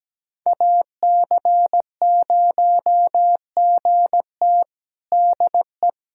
Morse code, heard as a keyed tone, about 17 wpm, 700 Hz. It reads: AC0GT DE
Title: USA Callsign DE Callsign - 5 Characters - 17wpm